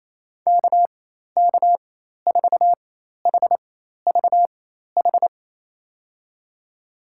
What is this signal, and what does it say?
Morse code, heard as a keyed tone, about 28 wpm, spaced as for 13 wpm, 700 Hz. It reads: KK4HVH
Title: USA Call Signs - 6 Characters - 4x Character Spacing - 28wpm